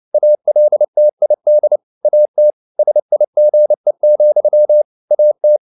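Morse code, heard as a keyed tone, about 29 wpm, 600 Hz. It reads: ALTID AT SIGE, AT
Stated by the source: Fast (QRQ) Morse at 29 wpm, from CW med Gnister.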